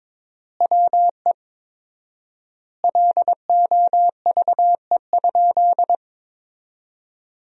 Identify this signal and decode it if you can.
Morse code, heard as a keyed tone, about 21 wpm, 700 Hz. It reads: WE LOVE?
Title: 4x Word-Spacing - Sentences from Top 400 Words - 22wpm